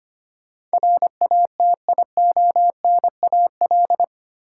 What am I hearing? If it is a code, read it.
Morse code, heard as a keyed tone, about 25 wpm, 700 Hz. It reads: RATIONAL
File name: Eight Letter Words - 25wpm